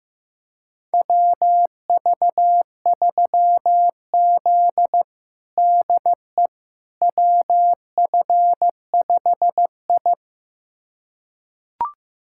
Morse code, heard as a keyed tone, about 15 wpm, 700 Hz. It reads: WV3Z DE WF5I
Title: USA Callsign DE Callsign - 4 Characters - 15wpm